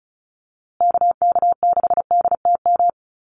Morse code, heard as a keyed tone, about 34 wpm, 700 Hz. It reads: KK6DTM
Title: USA Call Signs Repeated 3x 35wpm